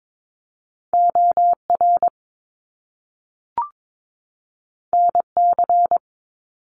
Morse code, heard as a keyed tone, about 22 wpm, 700 Hz. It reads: OR NC
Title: Top Two Letter Combinations in Words 22wpm